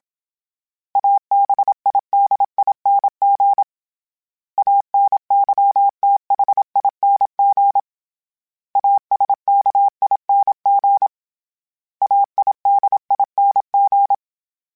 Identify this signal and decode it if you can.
Morse code, heard as a keyed tone, about 27 wpm, 800 Hz. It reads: ABIDING ANYTHING ASKING AIDING